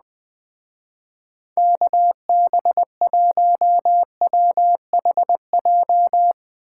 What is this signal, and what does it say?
Morse code, heard as a keyed tone, about 20 wpm, 700 Hz. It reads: KB1WHJ
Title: Speed Racing - USA 2x3 Call Signs 17wpm